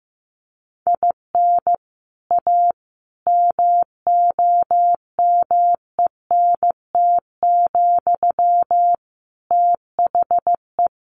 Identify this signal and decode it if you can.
Morse code, heard as a keyed tone, about 15 wpm, 700 Hz. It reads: IN A MOMENT, THE
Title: Sentences from Top 1000 Words 15wpm